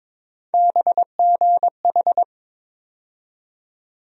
Morse code, heard as a keyed tone, about 21 wpm, 700 Hz. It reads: BGH
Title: USA Call Sign Postfixes - 3 Letters 22wpm